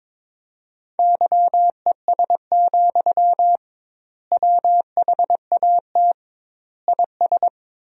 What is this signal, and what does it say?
Morse code, heard as a keyed tone, about 22 wpm, 700 Hz. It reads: YES, WHAT IS
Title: 2x Word-Spacing - Sentences from Top 500 Words - 22wpm